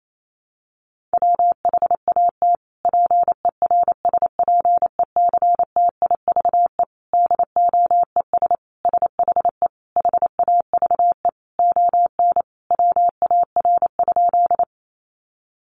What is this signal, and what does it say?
Morse code, heard as a keyed tone, about 28 wpm, 700 Hz. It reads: WHAT PERSPECTIVE DOES SHE HAVE ON WAR?
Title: Sentences from Top 1200 Words 28wpm